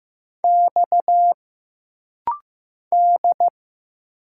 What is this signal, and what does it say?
Morse code, heard as a keyed tone, about 15 wpm, 700 Hz. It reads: X D
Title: Single Letter-Number Alt - Rapid-Fire 15wpm